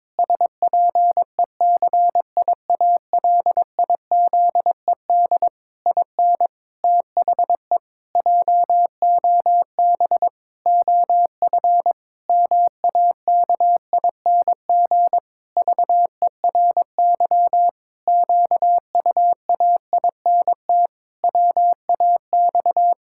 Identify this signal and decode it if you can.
Morse code, heard as a keyed tone, about 22 wpm, 700 Hz. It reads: SPECIALIZED IN THE JOB OF MAKING VERY QUAINT WAX
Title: Pangram Challenge v2 - 22wpm